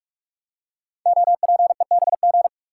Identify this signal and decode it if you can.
Morse code, heard as a keyed tone, about 44 wpm, 700 Hz. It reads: OPEBG
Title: Code Groups - 5 Letter:Numbers - 45wpm